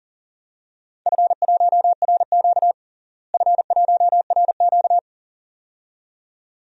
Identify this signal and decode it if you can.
Morse code, heard as a keyed tone, about 40 wpm, 700 Hz. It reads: F1RQ F1RQ
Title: DXCC-100 Call Signs - 4 Characters - Repeated 2x - 40wpm